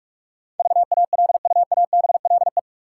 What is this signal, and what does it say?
Morse code, heard as a keyed tone, about 46 wpm, 700 Hz. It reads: VALUABLE
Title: Sentences from Top 1100 Words - New Word Drill Alt - 45wpm